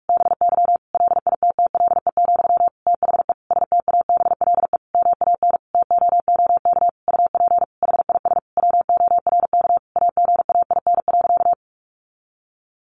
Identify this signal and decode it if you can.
Morse code, heard as a keyed tone, about 44 wpm, 700 Hz. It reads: BY LITTLE, THE STABLE MAN TOOK UP HIS WORK AGAIN.
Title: Sentences from Top 1500 Words 45wpm